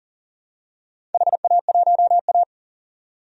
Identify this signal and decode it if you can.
Morse code, heard as a keyed tone, about 40 wpm, 700 Hz. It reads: HA1A
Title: DXCC-100 Call Signs - 4 Characters - 40wpm